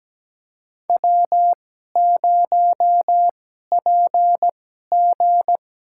Morse code, heard as a keyed tone, about 17 wpm, spaced as for 12 wpm, 700 Hz. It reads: W0PG
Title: USA Call Signs - 2x Character Spacing - 17wpm